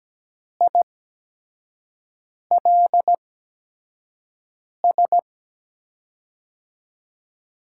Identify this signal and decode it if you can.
Morse code, heard as a keyed tone, about 17 wpm, spaced as for 5 wpm, 700 Hz. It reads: ILS
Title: USA Call Signs - 5 Characters - 8x Character Spacing - 17wpm